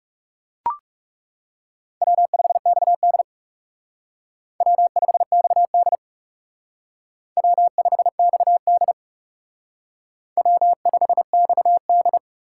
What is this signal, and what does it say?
Morse code, heard as a keyed tone, about 37 wpm, 700 Hz. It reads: W5XD W5XD W5XD W5XD
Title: Speed Racing - USA Call Signs 30wpm